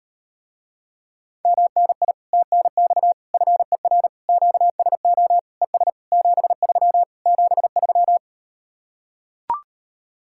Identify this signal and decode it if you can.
Morse code, heard as a keyed tone, about 38 wpm, 700 Hz. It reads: MNI TNX FER QSO ES 73 73